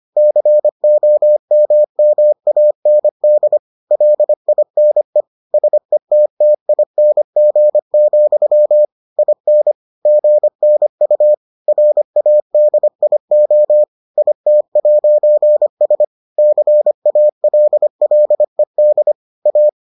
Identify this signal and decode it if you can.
Morse code, heard as a keyed tone, about 25 wpm, 600 Hz. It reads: COMMAND LINE SETTING, IN GNU RADIO IT'S CALLED A